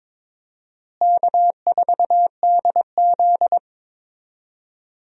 Morse code, heard as a keyed tone, about 22 wpm, 700 Hz. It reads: K4DZ
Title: USA Call Signs - 4 Characters - Repeated 2x - 22wpm